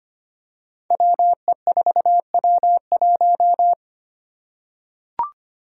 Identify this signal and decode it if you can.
Morse code, heard as a keyed tone, about 25 wpm, 700 Hz. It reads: WE4W1
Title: Five Challenging Letter-Number Combinations 25wpm